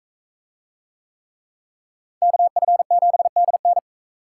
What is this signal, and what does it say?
Morse code, heard as a keyed tone, about 39 wpm, 700 Hz. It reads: KF7DN